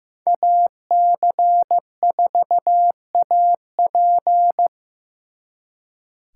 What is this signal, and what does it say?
Morse code, heard as a keyed tone, about 17 wpm, 700 Hz. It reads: AC4AP
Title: Speed Racing - USA 2x2 Call Signs 15wpm